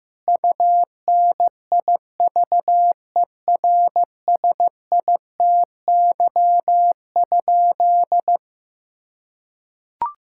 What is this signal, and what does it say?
Morse code, heard as a keyed tone, about 15 wpm, 700 Hz. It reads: UNIVERSITY?
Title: Sentences from Top 1000 Words - Review Part 5 - 15wpm